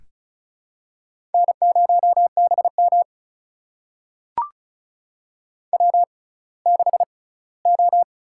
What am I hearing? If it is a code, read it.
Morse code, heard as a keyed tone, about 36 wpm, spaced as for 20 wpm, 700 Hz. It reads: N0BM W6O
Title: USA Call Signs - 4 Characters - 6x Character Spacing - 35wpm